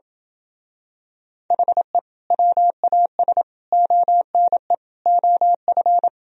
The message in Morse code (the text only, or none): HE WAS ONE OF